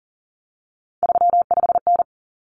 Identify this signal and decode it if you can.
Morse code, heard as a keyed tone, about 40 wpm, 700 Hz. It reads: 35N
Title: Common Cut Numbers - 3-Digit 40wpm